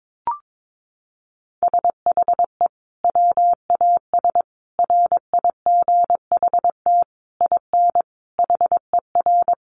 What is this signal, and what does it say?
Morse code, heard as a keyed tone, about 22 wpm, 700 Hz. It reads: SHE WAS RIGHT IN HER
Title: Sentences from Top 1500 Words 22wpm